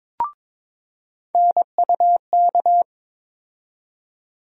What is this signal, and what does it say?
Morse code, heard as a keyed tone, about 22 wpm, 700 Hz. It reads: NUK
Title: USA Call Sign Postfixes - 3 Letters 22wpm